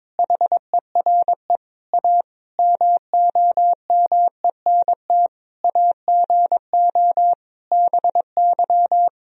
Morse code, heard as a keyed tone, about 22 wpm, 700 Hz. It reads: HERE A MOMENT AGO BY
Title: Sentences from Top 1000 Words - Review Part 3 - 22wpm